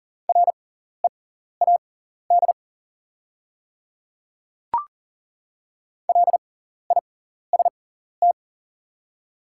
Morse code, heard as a keyed tone, about 40 wpm, spaced as for 14 wpm, 700 Hz. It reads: READ LIST
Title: 6x Character Spacing - Top 500 Words - 40wpm